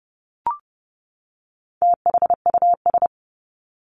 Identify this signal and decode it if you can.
Morse code, heard as a keyed tone, about 30 wpm, 700 Hz. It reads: THUS
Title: Top 751-1000 Words 30wpm